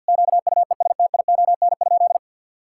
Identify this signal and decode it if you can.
Morse code, heard as a keyed tone, about 50 wpm, 700 Hz. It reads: QUESTION?
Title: Sentences from Top 1000 Words - Review Part 5 - 50wpm